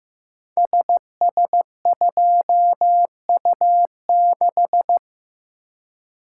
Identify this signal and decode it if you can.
Morse code, heard as a keyed tone, about 15 wpm, 700 Hz. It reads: SS2U6
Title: Code Groups - 5 Letter:Numbers - 15wpm